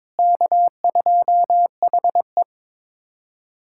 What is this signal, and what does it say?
Morse code, heard as a keyed tone, about 21 wpm, 700 Hz. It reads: K2HE
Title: USA Callsign DE Callsign - 4 Characters - 22wpm